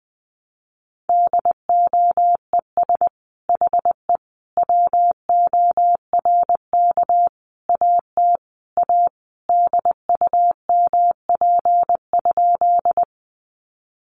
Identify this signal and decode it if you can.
Morse code, heard as a keyed tone, about 20 wpm, 700 Hz. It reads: DOES HE WORK AT A DUMP?
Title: Sentences from Top 1300 Words 20wpm